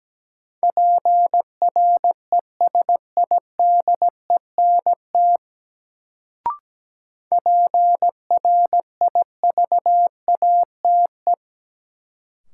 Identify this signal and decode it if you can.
Morse code, heard as a keyed tone, about 17 wpm, 700 Hz. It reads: PRESIDENT PRIVATE
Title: Sentences from Top 1000 Words - New Word Drill - 17wpm